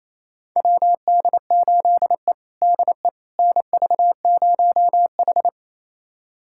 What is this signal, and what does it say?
Morse code, heard as a keyed tone, about 28 wpm, 700 Hz. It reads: WD8E DE NV0H
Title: USA Callsign DE Callsign - 4 Characters - 28wpm